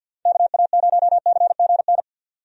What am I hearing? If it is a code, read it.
Morse code, heard as a keyed tone, about 52 wpm, 700 Hz. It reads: KA0CGN
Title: USA Call Signs - 6 Characters - Repeated 3x - 50wpm